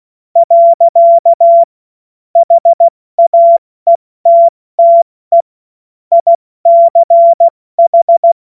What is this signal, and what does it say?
Morse code, heard as a keyed tone, about 16 wpm, 680 Hz. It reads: . HAETTE ICH
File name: Morsekurs / Witze / Witz 18. Juli 2023 70 F80